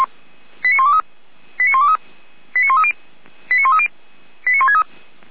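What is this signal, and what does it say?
Five-tone selective call (ZVEI), heard as quick runs of five tones, tones of about 70 ms. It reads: 89123 89123 89130 89130 89163
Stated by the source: ZVEI five-tone selective call 89119, received by SDR.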